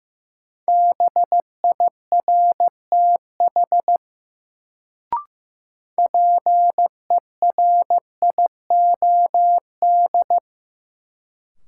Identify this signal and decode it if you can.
Morse code, heard as a keyed tone, about 15 wpm, 700 Hz. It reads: BIRTH PERIOD
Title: Sentences from Top 800 Words - New Word Drill - 15wpm